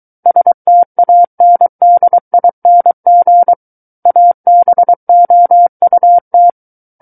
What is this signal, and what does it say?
Morse code, heard as a keyed tone, about 23 wpm, 700 Hz. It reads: STANDING ABOUT
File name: Book One / 14-Chapter 14 - 23 WPM